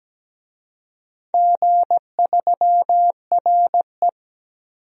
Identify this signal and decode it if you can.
Morse code, heard as a keyed tone, about 17 wpm, 700 Hz. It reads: G3RE
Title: DXCC-100 Call Signs - 4 Characters - 17wpm